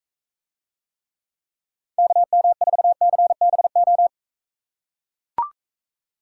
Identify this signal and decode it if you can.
Morse code, heard as a keyed tone, about 43 wpm, 700 Hz. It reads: KM4CBO